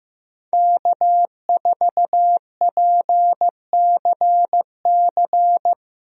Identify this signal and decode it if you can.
Morse code, heard as a keyed tone, about 15 wpm, 700 Hz. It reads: K4PCC
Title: USA Call Signs - 5 Characters - Repeated 3x - 15wpm